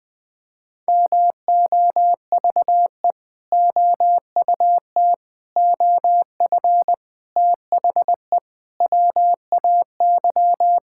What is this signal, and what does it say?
Morse code, heard as a keyed tone, about 21 wpm, 700 Hz. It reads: MOVE OUT OF THE WAY
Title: Sentences with 4-Word Phrases - 20wpm